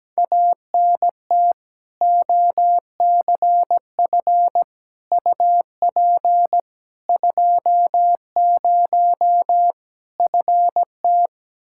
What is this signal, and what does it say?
Morse code, heard as a keyed tone, about 17 wpm, 700 Hz. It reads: ANT OCF UP 20 FT
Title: Practice QSO - Exchange 2 - Info - 17wpm